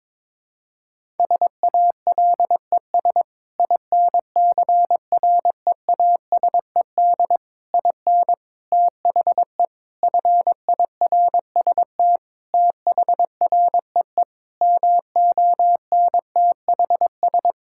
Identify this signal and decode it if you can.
Morse code, heard as a keyed tone, about 22 wpm, 700 Hz. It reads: SALES INCREASED IN THE FIRST THREE MONTHS
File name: practice / 22wpm / Sentences with 8-Word Phrases - 22wpm